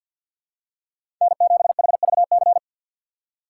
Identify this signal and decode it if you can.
Morse code, heard as a keyed tone, about 49 wpm, 700 Hz. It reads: N7HVC
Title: USA Call Signs - 5 Characters - 4x Character Spacing - 50wpm